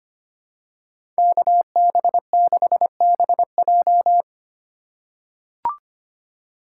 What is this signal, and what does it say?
Morse code, heard as a keyed tone, about 25 wpm, 700 Hz. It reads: KB6BJ
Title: USA Call Signs - 5 Characters - 3x Character Spacing - 25wpm